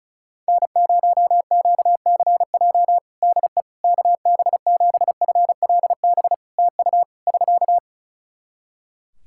Morse code, heard as a keyed tone, about 34 wpm, 700 Hz. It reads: N0QCJ DE KB7FLB TU <SK>
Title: Practice QSO - Exchange 4 - Ending - 35wpm